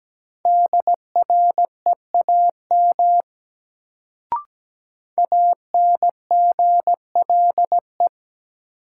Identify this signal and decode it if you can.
Morse code, heard as a keyed tone, about 17 wpm, 700 Hz. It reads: DREAM ANGLE
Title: Sentences from Top 1500 Words - New Word Drill Alt - 17wpm